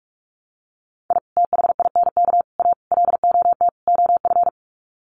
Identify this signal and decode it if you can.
Morse code, heard as a keyed tone, about 44 wpm, 700 Hz. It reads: I THINK A LOT OF
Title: Sets of 5 Words 45wpm